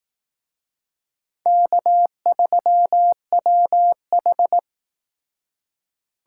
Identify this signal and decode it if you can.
Morse code, heard as a keyed tone, about 19 wpm, 700 Hz. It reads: K3WH
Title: Speed Racing - USA 1x2 Call Signs 15wpm